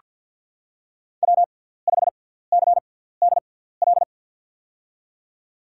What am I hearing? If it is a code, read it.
Morse code, heard as a keyed tone, about 48 wpm, spaced as for 17 wpm, 700 Hz. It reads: W5CDL
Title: USA Call Signs - 5 Characters - 6x Character Spacing - 50wpm